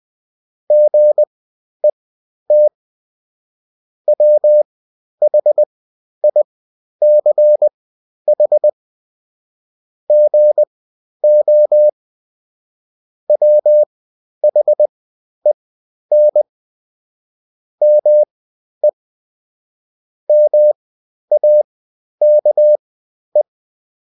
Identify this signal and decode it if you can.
Morse code, heard as a keyed tone, about 20 wpm, spaced as for 11 wpm, 600 Hz. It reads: GET WHICH GO WHEN ME MAKE
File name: Top-100-Words-10